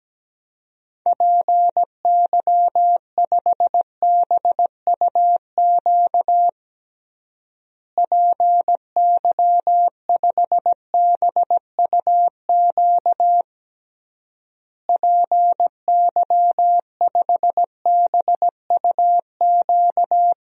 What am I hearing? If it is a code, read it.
Morse code, heard as a keyed tone, about 17 wpm, 700 Hz. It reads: PY5BUQ PY5BUQ PY5BUQ
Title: DXCC-100 Call Signs - 6 Characters - Repeated 3x - 17wpm